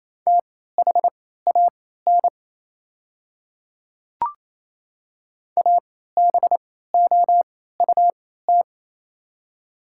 Morse code, heard as a keyed tone, about 28 wpm, spaced as for 16 wpm, 690 Hz. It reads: THAN ABOUT
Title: 3x Character Spacing - Top 500 Words - 28wpm